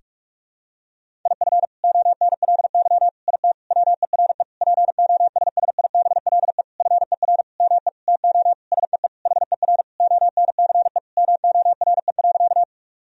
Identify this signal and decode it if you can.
Morse code, heard as a keyed tone, about 46 wpm, 690 Hz. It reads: IF ONLY IT WERE POSSIBLE FER ME TO SEE HER ONCE MORE.
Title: Sentences from Top 500 Words and Abbreviations 45wpm